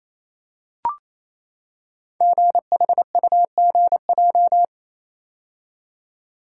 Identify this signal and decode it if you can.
Morse code, heard as a keyed tone, about 28 wpm, 700 Hz. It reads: GHUGJ
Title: Code Groups - 5 Letters - 28wpm